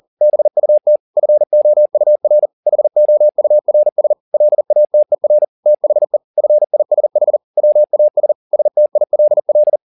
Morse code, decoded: BUT FOUR HOURS LATER THE FISH WAS STILL

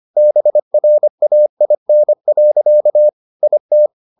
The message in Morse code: BRAIN. IT

